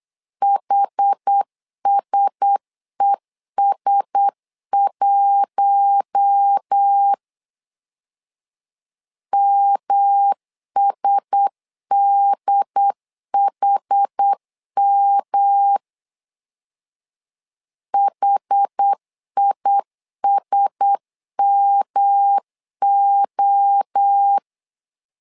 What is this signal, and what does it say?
Morse code, heard as a keyed tone, about 8 wpm, 800 Hz. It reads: HSES1 MSDHM HISMO
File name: LEZIONI / 4 Q1S 8wpm 2710